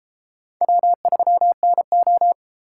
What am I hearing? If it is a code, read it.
Morse code, heard as a keyed tone, about 30 wpm, 700 Hz. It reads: W3NO